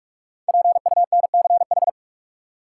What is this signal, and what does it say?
Morse code, heard as a keyed tone, about 46 wpm, 700 Hz. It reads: PUNCH
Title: Five Letter Words - 45wpm